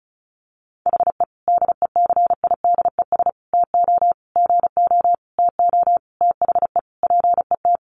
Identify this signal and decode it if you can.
Morse code, heard as a keyed tone, about 34 wpm, 700 Hz. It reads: HE DECIDES TO GO TO THE PET
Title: Easy Phrases 35wpm